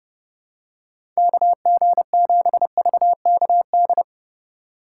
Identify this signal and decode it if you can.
Morse code, heard as a keyed tone, about 32 wpm, 700 Hz. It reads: KG7VKD
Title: USA Call Signs - 2x Character Spacing - 30wpm